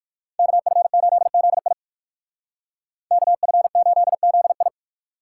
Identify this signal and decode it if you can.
Morse code, heard as a keyed tone, about 54 wpm, 700 Hz. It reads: KF8ZI KF8ZI
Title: Speed Racing - USA Call Signs 45wpm